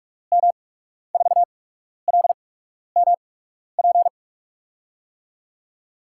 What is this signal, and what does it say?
Morse code, heard as a keyed tone, about 47 wpm, spaced as for 12 wpm, 700 Hz. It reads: M4LMP
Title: USA Call Signs - 6 Characters - 8x Character Spacing - 45wpm